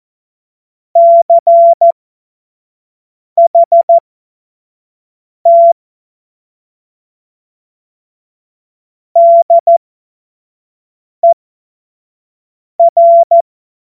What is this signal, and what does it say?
Morse code, heard as a keyed tone, about 14 wpm, spaced as for 5 wpm, 680 Hz. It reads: CHT DER
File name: Morsekurs / Witze / Witz 01. März 2023 25 F70